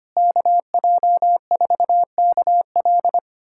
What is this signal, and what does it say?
Morse code, heard as a keyed tone, about 25 wpm, 700 Hz. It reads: KJ4KL
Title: USA Callsign DE Callsign - 5 Characters - 25wpm